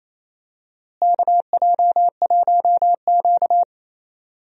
Morse code, heard as a keyed tone, about 28 wpm, 700 Hz. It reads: KJ1Q
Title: USA Call Signs - 4 Characters - Repeated 2x - 28wpm